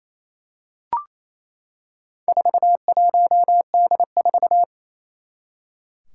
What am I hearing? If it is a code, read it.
Morse code, heard as a keyed tone, about 28 wpm, 700 Hz. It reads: 41D4
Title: Four Challenging Letter-Number Combinations 28wpm